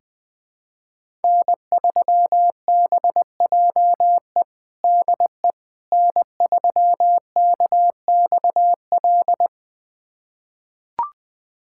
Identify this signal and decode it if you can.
Morse code, heard as a keyed tone, about 20 wpm, 700 Hz. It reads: N3BJE DE N3KXL
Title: USA Callsign DE Callsign - 5 Characters - 20wpm